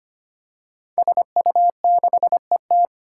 Morse code, heard as a keyed tone, about 25 wpm, 700 Hz. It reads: SU6ET
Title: Five Challenging Letter-Number Combinations 25wpm